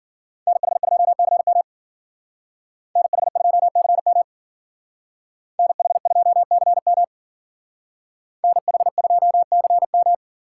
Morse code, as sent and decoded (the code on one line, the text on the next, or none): -. .... ..--- -.-. -- / -. .... ..--- -.-. -- / -. .... ..--- -.-. -- / -. .... ..--- -.-. --
NH2CM NH2CM NH2CM NH2CM